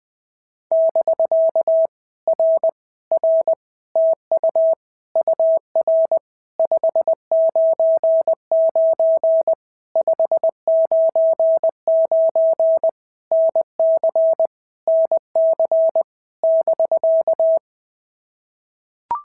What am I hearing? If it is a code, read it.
Morse code, heard as a keyed tone, about 19 wpm, 650 Hz. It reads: <BK> R R TU UR 599 599 NC NC <BK>
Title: POTA QSO Practice - 20wpm